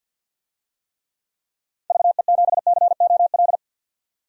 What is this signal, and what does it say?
Morse code, heard as a keyed tone, about 52 wpm, 700 Hz. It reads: VE7COL
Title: DXCC-100 Call Signs - 6 Characters - 50wpm